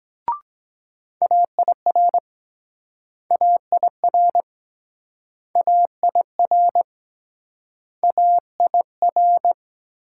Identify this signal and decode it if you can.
Morse code, heard as a keyed tone, about 23 wpm, 700 Hz. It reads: AIR AIR AIR AIR